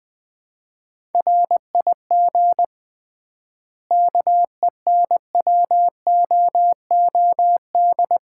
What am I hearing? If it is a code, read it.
Morse code, heard as a keyed tone, about 21 wpm, 700 Hz. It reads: RIG KENWOOD
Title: Common QSO Phrases - 3x Word-Spacing - 20wpm